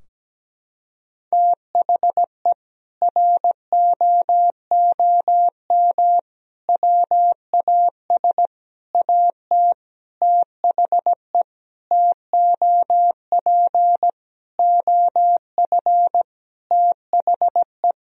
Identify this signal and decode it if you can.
Morse code, heard as a keyed tone, about 17 wpm, 700 Hz. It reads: THE ROOM WAS AT THE TOP OF THE
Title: Sentences from Top 1000 Words - Review Part 2 - 17wpm